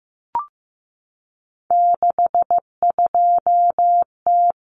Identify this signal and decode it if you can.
Morse code, heard as a keyed tone, about 15 wpm, 700 Hz. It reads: 62T
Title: Common Cut Numbers - 3-Digit 15wpm